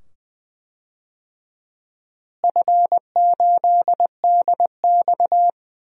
Morse code, heard as a keyed tone, about 20 wpm, 700 Hz. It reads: F8DX
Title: DXCC-100 Call Signs - 4 Characters - Repeated 2x - 20wpm